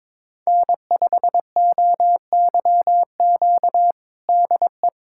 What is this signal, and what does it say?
Morse code, heard as a keyed tone, about 21 wpm, 700 Hz. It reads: N5OYQ DE
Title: USA Callsign DE Callsign 22wpm